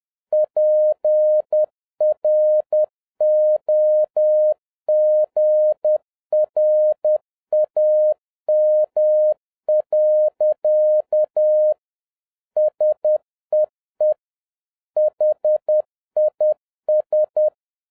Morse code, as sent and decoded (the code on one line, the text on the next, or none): .--. .-. --- --. .-. .- -- .-.-.- / ... . / .... .. ...
PROGRAM. SEE HIS